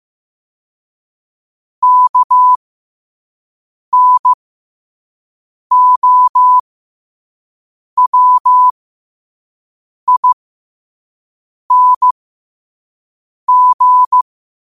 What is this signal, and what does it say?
Morse code, heard as a keyed tone, about 15 wpm, spaced as for 5 wpm, 1000 Hz. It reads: KNOWING
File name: qotdfiles / 5wpm